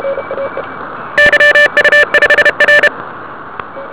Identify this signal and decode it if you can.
Morse code, heard as a keyed tone, about 31 wpm, 1750 Hz, with noise. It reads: YU5R
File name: ARCHIV / 80m cw